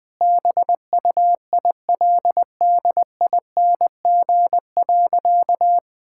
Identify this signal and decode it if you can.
Morse code, heard as a keyed tone, about 20 wpm, 700 Hz. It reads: BUILDING.
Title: Sentences from Top 1000 Words - Review Part 5 - 20wpm